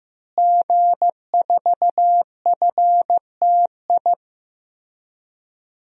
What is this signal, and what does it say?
Morse code, heard as a keyed tone, about 15 wpm, 700 Hz. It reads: G4FTI
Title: DXCC-100 Call Signs - 5 Characters - Repeated 3x - 15wpm